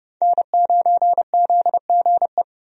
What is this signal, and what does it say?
Morse code, heard as a keyed tone, about 30 wpm, 700 Hz. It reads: N9ZGE
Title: USA Call Signs - 5 Characters - 30wpm